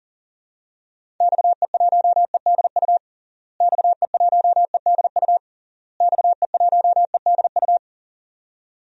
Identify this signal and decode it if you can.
Morse code, heard as a keyed tone, about 40 wpm, 700 Hz. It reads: XE1EDU XE1EDU XE1EDU
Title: DXCC-100 Call Signs - 6 Characters - Repeated 3x - 40wpm